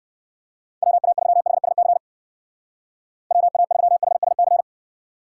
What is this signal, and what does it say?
Morse code, heard as a keyed tone, about 55 wpm, 700 Hz. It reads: WA3HSC WA3HSC
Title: Speed Racing - USA 2x3 Call Signs 45wpm